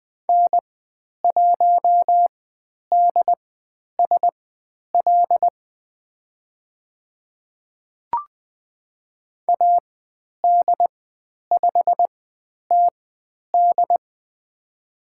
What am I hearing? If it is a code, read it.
Morse code, heard as a keyed tone, about 20 wpm, spaced as for 10 wpm, 700 Hz. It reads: N1DSL AD5TD
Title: 095 Lesson 16g - Letter c - Call Signs 20wpm